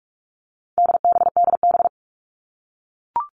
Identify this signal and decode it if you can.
Morse code, heard as a keyed tone, about 44 wpm, 700 Hz. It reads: DBDB
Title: All Cut Numbers - 4-Digit 45wpm